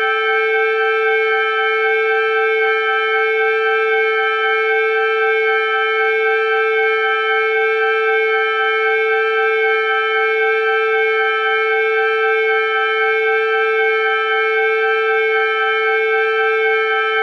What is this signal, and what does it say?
Emergency Alert System (SAME) tones (header bursts), heard as harsh buzzing bursts. It is heard from 0.7 to 16.4 s.